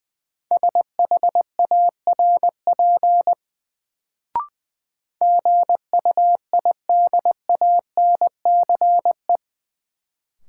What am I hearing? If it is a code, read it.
Morse code, heard as a keyed tone, about 21 wpm, 700 Hz. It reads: SHARP GUIDANCE
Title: Sentences from Top 1600 Words - New Word Drill - 20wpm